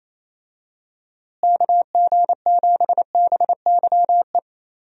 Morse code, heard as a keyed tone, about 28 wpm, 700 Hz. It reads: KG7BYE
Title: USA Call Signs - 6 Characters - Repeated 2x - 28wpm